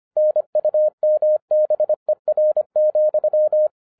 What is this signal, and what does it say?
Morse code, heard as a keyed tone, about 24 wpm, 610 Hz. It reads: NUMBER,
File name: FIRST FISTS CONTACT-25wpm